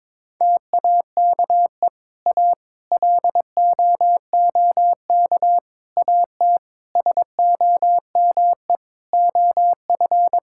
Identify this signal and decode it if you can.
Morse code, heard as a keyed tone, about 22 wpm, 700 Hz. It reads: TAKE A LOOK AT SOME OF
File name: practice / 22wpm / Sets of 7 Words - 22wpm